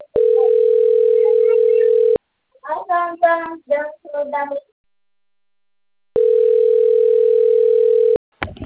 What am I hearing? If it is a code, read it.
Ringback tone (North America), 440 + 480 Hz.
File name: out- - - - - .29986